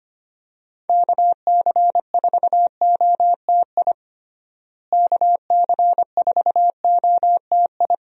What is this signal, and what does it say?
Morse code, heard as a keyed tone, about 25 wpm, 700 Hz. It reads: KC4OTI KC4OTI
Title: USA Call Signs - 6 Characters - Repeated 2x - 25wpm